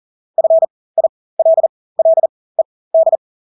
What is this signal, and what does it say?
Morse code, heard as a keyed tone, about 40 wpm, spaced as for 20 wpm, 650 Hz. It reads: FILLED
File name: Engelske ord 40-20 wpm